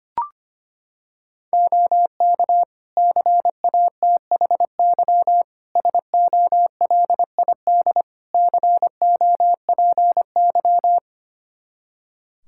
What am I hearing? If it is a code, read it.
Morse code, heard as a keyed tone, about 25 wpm, 700 Hz. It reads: OK CATHY SOLID COPY
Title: K1USN SST QSO Practice 25wpm